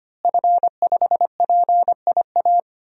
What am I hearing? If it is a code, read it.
Morse code, heard as a keyed tone, about 25 wpm, 700 Hz. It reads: F5PIA
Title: DXCC-100 Call Signs - 5 Characters - Repeated 3x - 25wpm